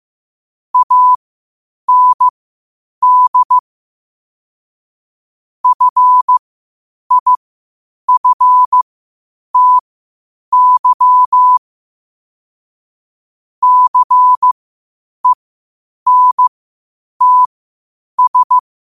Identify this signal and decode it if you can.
Morse code, heard as a keyed tone, about 15 wpm, spaced as for 8 wpm, 1000 Hz. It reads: AND FIFTY CENTS
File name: qotdfiles / 8wpm